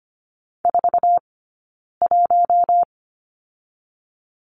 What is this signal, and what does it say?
Morse code, heard as a keyed tone, about 25 wpm, spaced as for 9 wpm, 700 Hz. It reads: 41
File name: course / 25:10wpm / 099 Lesson 17d - Letter u - Two Characters 25wpm